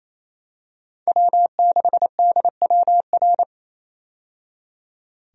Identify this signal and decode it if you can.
Morse code, heard as a keyed tone, about 28 wpm, 700 Hz. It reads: W6DWR
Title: USA Call Signs - 5 Characters - 28wpm